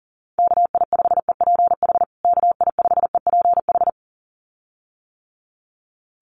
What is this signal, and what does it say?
Morse code, heard as a keyed tone, about 40 wpm, 700 Hz. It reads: KI5EPH KI5EPH
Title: USA Call Signs Repeated 2x 40wpm